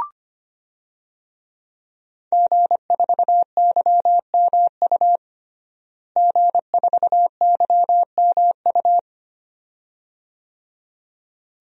Morse code, heard as a keyed tone, about 25 wpm, 700 Hz. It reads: G4YMU G4YMU
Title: DXCC-100 Call Signs - 5 Characters - Repeated 2x - 25wpm